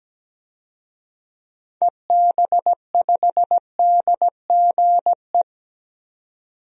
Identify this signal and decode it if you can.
Morse code, heard as a keyed tone, about 17 wpm, 700 Hz. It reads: EB5DGE
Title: DXCC-100 Call Signs - 6 Characters - 17wpm